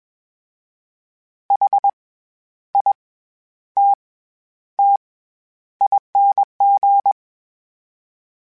Morse code, heard as a keyed tone, about 21 wpm, 800 Hz. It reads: H I T T ING
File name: Special INGs-mp3s / Special INGs-mp3s / SlowToFaster ing @20